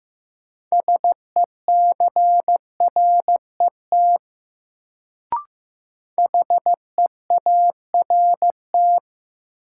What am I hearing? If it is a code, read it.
Morse code, heard as a keyed tone, about 15 wpm, 700 Hz. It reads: SECRET HEART
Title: Sentences from Top 1200 Words - New Word Drill Alt - 15wpm